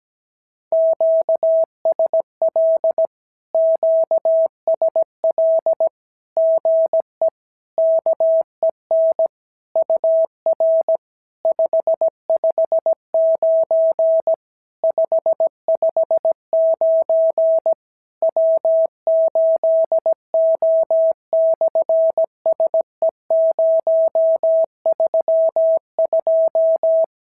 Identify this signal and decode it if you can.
Morse code, heard as a keyed tone, about 17 wpm, 650 Hz. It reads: QSL QSL GE KEN UR 559 559 W8O/SE032
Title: SOTA QSO Practice 17wpm